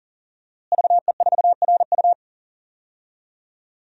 Morse code, heard as a keyed tone, about 40 wpm, 700 Hz. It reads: VE4RU
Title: DXCC-100 Call Signs - 5 Characters - Repeated 2x - 40wpm